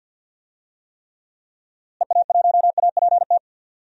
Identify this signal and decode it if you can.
Morse code, heard as a keyed tone, about 51 wpm, 700 Hz. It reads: EA1APT
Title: DXCC-100 Call Signs - 6 Characters - Repeated 3x - 50wpm